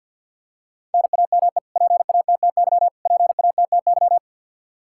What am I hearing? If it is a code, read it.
Morse code, heard as a keyed tone, about 51 wpm, 700 Hz. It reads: NAME PATTY PATTY
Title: Practice QSO - Exchange 1 - Beginning - 50wpm